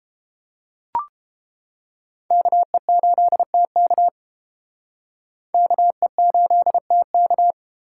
Morse code, heard as a keyed tone, about 32 wpm, 700 Hz. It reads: KE8TK KE8TK